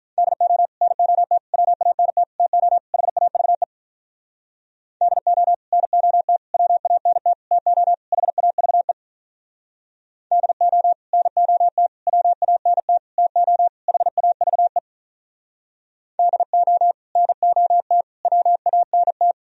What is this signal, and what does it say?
Morse code, heard as a keyed tone, about 46 wpm, 700 Hz. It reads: DO NOT WANT TO HAVE DO NOT WANT TO HAVE DO NOT WANT TO HAVE DO NOT WANT